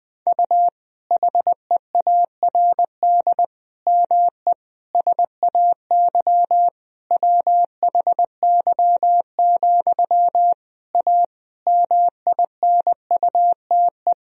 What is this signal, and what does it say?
Morse code, heard as a keyed tone, about 20 wpm, 700 Hz. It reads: U HEARD ME SAY WHY, A MINUTE
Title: Sentences from Top 500 Words and Abbreviations 20wpm